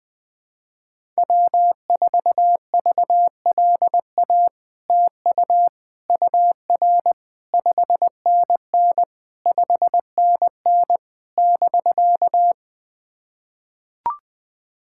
Morse code, heard as a keyed tone, about 20 wpm, 700 Hz. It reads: W4VLA TU UR 5NN 5NN <BK>
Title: SOTA QSO Practice 20wpm